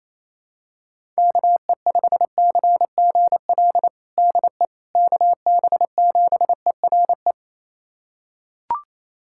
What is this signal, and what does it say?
Morse code, heard as a keyed tone, about 28 wpm, 700 Hz. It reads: KE5CGL DE KB7ERE
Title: USA Callsign DE Callsign - 6 Characters - 28wpm